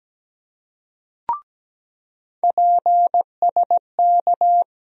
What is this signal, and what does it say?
Morse code, heard as a keyed tone, about 17 wpm, 700 Hz. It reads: PSK
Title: USA Call Sign Postfixes - 3 Letters 17wpm